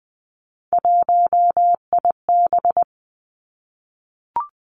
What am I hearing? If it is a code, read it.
Morse code, heard as a keyed tone, about 20 wpm, 700 Hz. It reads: 1IB
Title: Three Challenging Letter-Number Combinations 20wpm